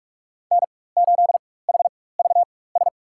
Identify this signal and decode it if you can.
Morse code, heard as a keyed tone, about 46 wpm, spaced as for 21 wpm, 700 Hz. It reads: N8HVS
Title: USA Call Signs - 4x Character Spacing - 45wpm